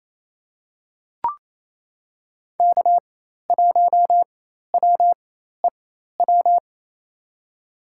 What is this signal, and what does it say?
Morse code, heard as a keyed tone, about 28 wpm, spaced as for 13 wpm, 700 Hz. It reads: K1WEW
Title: USA Call Signs - 5 Characters - 4x Character Spacing - 28wpm